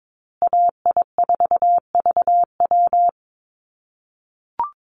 Morse code, heard as a keyed tone, about 22 wpm, 700 Hz. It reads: AI4VW
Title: USA Call Signs Repeated 3x 22wpm